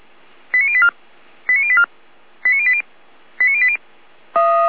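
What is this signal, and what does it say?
Five-tone selective call (ZVEI), heard as quick runs of five tones, tones of about 70 ms. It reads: 89084 89084 89080 89080
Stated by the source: ZVEI five-tone selective call received by SDR.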